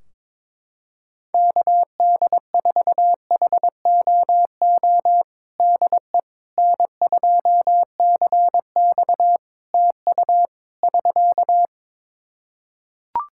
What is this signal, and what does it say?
Morse code, heard as a keyed tone, about 22 wpm, 700 Hz. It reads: KD4HOO DE N2CX TU <SK>
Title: Practice QSO - Exchange 4 - Ending - 22wpm